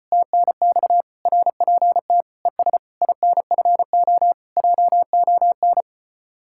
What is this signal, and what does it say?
Morse code, heard as a keyed tone, about 32 wpm, 700 Hz. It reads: TNX RPT ES INFO JON